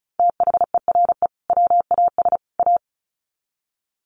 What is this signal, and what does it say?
Morse code, heard as a keyed tone, about 34 wpm, 700 Hz. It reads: THERE WAS A
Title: Sets of 3 Words 35wpm